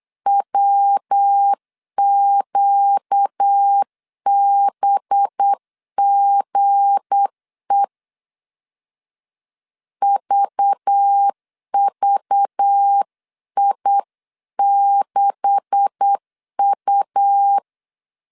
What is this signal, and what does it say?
Morse code, heard as a keyed tone, about 8 wpm, 800 Hz. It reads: WQBGE VVI6U
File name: CW / DET / 1 9 1500C S C 8wpm 5653